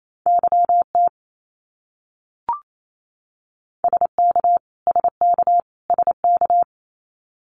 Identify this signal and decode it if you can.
Morse code, heard as a keyed tone, about 28 wpm, 700 Hz. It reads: YT SK SK SK
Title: Canadian Provinces and Territories - Repeated 3x - 28wpm